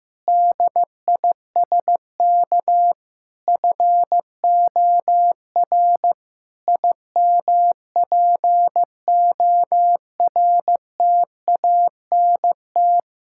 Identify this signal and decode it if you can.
Morse code, heard as a keyed tone, about 14 wpm, 700 Hz. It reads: DISK FOR IMPORTANT
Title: Sentences from Top 1000 Words - Review Part 6 - 15wpm